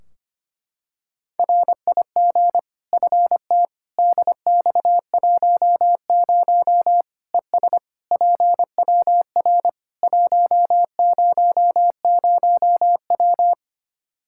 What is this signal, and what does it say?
Morse code, heard as a keyed tone, about 25 wpm, 700 Hz. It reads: RIG FT DX10 ES PWR 100W
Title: Practice QSO - Exchange 2 - Info - 25wpm